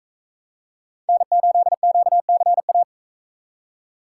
Morse code, heard as a keyed tone, about 41 wpm, 700 Hz. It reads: N8QCA